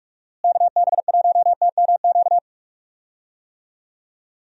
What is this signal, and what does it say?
Morse code, heard as a keyed tone, about 42 wpm, 700 Hz. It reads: KB1TMQ
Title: USA Call Signs - 6 Characters - Repeated 2x - 45wpm